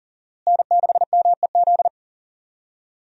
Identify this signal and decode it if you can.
Morse code, heard as a keyed tone, about 40 wpm, 700 Hz. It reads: N6MEZ
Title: USA Call Signs - 5 Characters - Repeated 2x - 40wpm